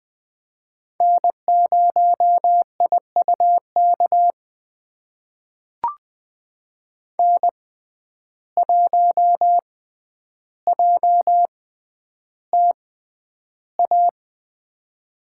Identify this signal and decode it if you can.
Morse code, heard as a keyed tone, about 20 wpm, spaced as for 10 wpm, 700 Hz. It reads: N0IUK N1JTA
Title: USA Call Signs - 5 Characters - 6x Character Spacing - 20wpm